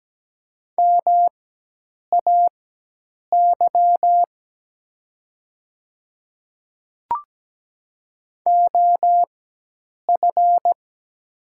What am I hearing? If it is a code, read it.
Morse code, heard as a keyed tone, about 17 wpm, spaced as for 8 wpm, 700 Hz. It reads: MAY OF
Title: 4x Character Spacing - Top 100 Words - 17wpm